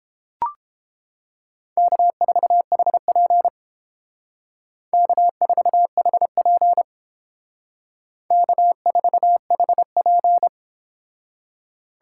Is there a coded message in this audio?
Yes: Morse code from 1.8 to 10.5 s, heard as a keyed tone, about 30 wpm, 700 Hz. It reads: K4HP K4HP K4HP